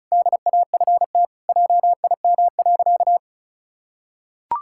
Morse code, heard as a keyed tone, about 34 wpm, 700 Hz. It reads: DAFT JIM.
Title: Pangram Challenge v2 - 35wpm